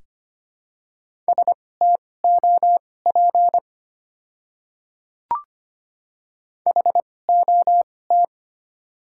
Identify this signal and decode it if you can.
Morse code, heard as a keyed tone, about 25 wpm, spaced as for 18 wpm, 700 Hz. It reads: STOP HOT
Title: 2x Character Spacing - Top 500 Words - 25wpm